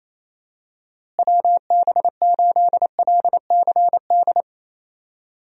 Morse code, heard as a keyed tone, about 28 wpm, 700 Hz. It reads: WB8LCD
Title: USA Call Signs - 6 Characters - Repeated 2x - 28wpm